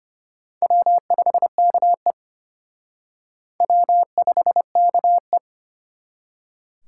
Morse code, heard as a keyed tone, about 31 wpm, 700 Hz. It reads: W5KE W5KE
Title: Speed Racing - USA 1x2 Call Signs 25wpm